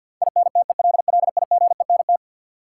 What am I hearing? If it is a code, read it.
Morse code, heard as a keyed tone, about 47 wpm, 700 Hz. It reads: INTELLIGENT
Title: Sentences from Top 1000 Words - Review Part 7 - New Word Drill - 50wpm